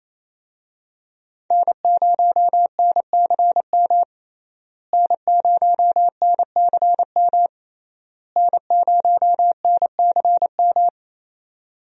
Morse code, heard as a keyed tone, about 27 wpm, 700 Hz. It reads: N0NCM N0NCM N0NCM
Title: USA Call Signs - 5 Characters - Repeated 3x - 28wpm